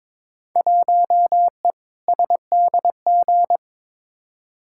Morse code, heard as a keyed tone, about 22 wpm, 700 Hz. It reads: 1E SDG
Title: Field Day Exchanges - 22wpm